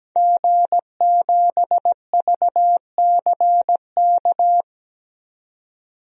Morse code, heard as a keyed tone, about 17 wpm, 700 Hz. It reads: G7VCK
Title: DXCC-100 Call Signs - 5 Characters - 17wpm